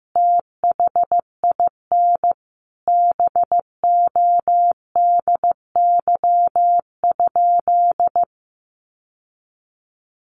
Morse code, heard as a keyed tone, about 15 wpm, 700 Hz. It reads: THIN BODY?
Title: Sentences from Top 1200 Words 15wpm